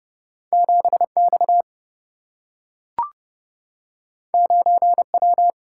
Morse code, heard as a keyed tone, about 29 wpm, 700 Hz. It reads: 7X 9W
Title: Single-Letter Single-Number 30wpm